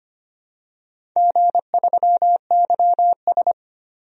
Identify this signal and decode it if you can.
Morse code, heard as a keyed tone, about 25 wpm, 700 Hz. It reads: G3YS
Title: DXCC-100 Call Signs - 4 Characters - Repeated 2x - 25wpm